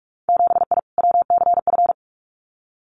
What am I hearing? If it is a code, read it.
Morse code, heard as a keyed tone, about 46 wpm, 700 Hz. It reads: 7I WCF
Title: Winter Field Day Exchanges Encore 45wpm